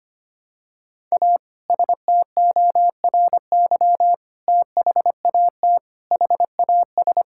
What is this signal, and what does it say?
Morse code, heard as a keyed tone, about 25 wpm, 700 Hz. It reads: A STORY THAT HAS
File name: practice / 25wpm / Sentences with 2-Word Phrases - 25wpm